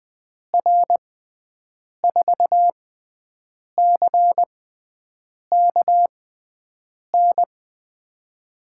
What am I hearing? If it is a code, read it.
Morse code, heard as a keyed tone, about 20 wpm, spaced as for 7 wpm, 700 Hz. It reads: R4CKN
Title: USA Call Signs - 6 Characters - 6x Character Spacing - 20wpm